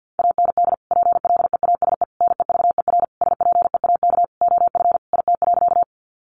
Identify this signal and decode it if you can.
Morse code, heard as a keyed tone, about 50 wpm, 700 Hz. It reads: AND PLEASE NEVER SPEAK OF IT.
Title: Sentences from Top 1500 Words 50wpm